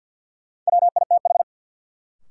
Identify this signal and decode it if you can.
Morse code, heard as a keyed tone, about 48 wpm, 700 Hz. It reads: WITH
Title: Sentences from Top 100 Words - New Word Drill - 50wpm